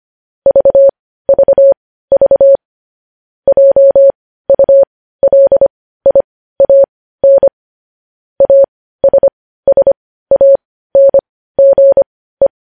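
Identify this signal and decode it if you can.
Morse code, heard as a keyed tone, about 25 wpm, spaced as for 15 wpm, 550 Hz. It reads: VVV JULIAN ASSANGE
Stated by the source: Morse news headlines practice at 15.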